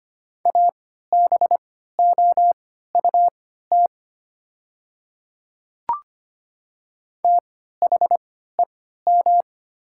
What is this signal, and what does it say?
Morse code, heard as a keyed tone, about 25 wpm, spaced as for 14 wpm, 700 Hz. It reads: ABOUT THEM
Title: 3x Character Spacing - Top 100 Words - 25wpm